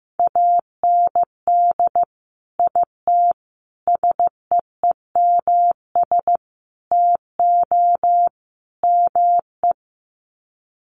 Morse code, heard as a keyed tone, about 15 wpm, 700 Hz. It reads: AND IT SEEMS TO ME
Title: Sets of 5 Words Encore2 15wpm